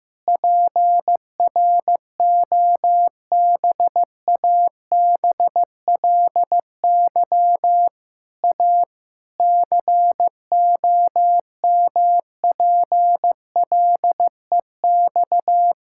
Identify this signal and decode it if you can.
Morse code, heard as a keyed tone, about 15 wpm, 700 Hz. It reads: PROBABLY A COMPLEX
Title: Sentences from Top 1000 Words - Review Part 6 - 15wpm